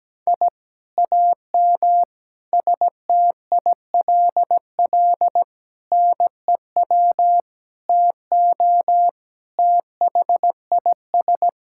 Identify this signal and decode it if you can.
Morse code, heard as a keyed tone, about 17 wpm, 700 Hz. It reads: I AM STILL NEW TO THIS
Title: Sentences from Top 1000 Words - Review Part 6 - 17wpm